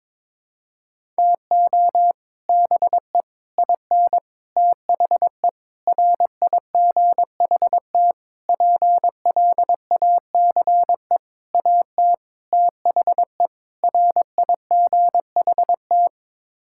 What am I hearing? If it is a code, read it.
Morse code, heard as a keyed tone, about 21 wpm, 700 Hz. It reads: TO BE IN THE RIGHT PLACE AT THE RIGHT
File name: Sets of 9 Words - 22wpm